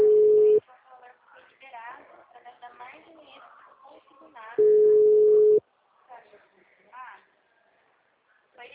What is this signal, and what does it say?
Ringback tone (Europe, 425 Hz), with speech.